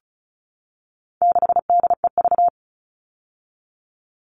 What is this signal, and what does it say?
Morse code, heard as a keyed tone, about 34 wpm, 700 Hz. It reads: 6DEV
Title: All Cut Numbers - 4-Digit 35wpm